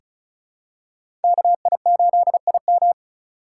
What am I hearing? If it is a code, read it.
Morse code, heard as a keyed tone, about 34 wpm, 700 Hz. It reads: KI8IM
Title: USA Call Signs - 5 Characters - 2x Character Spacing - 35wpm